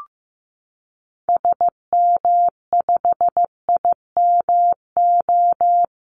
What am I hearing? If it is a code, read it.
Morse code, heard as a keyed tone, about 15 wpm, 700 Hz. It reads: SM5IMO
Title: Call Signs Repeated 3x - 15wpm